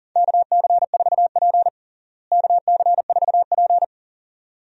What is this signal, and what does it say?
Morse code, heard as a keyed tone, about 37 wpm, 700 Hz. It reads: KC4P KC4P
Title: USA Call Signs - 4 Characters - Repeated 2x - 40wpm